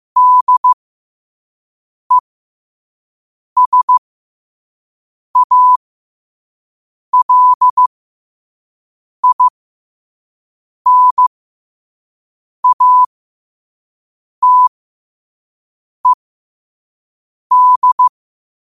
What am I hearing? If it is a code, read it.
Morse code, heard as a keyed tone, about 15 wpm, spaced as for 5 wpm, 1000 Hz. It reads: DESALINATED
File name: qotdfiles / 5wpm